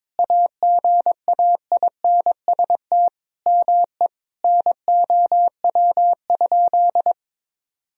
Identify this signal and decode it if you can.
Morse code, heard as a keyed tone, about 21 wpm, 700 Hz. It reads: AGAINST ME NOW?
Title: Sentences from Top 400 Words and Abbreviations 22wpm